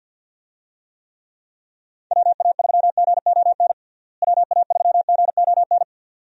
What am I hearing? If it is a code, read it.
Morse code, heard as a keyed tone, about 50 wpm, 700 Hz. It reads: WA3GON WA3GON
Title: USA Call Signs - 6 Characters - Repeated 2x - 50wpm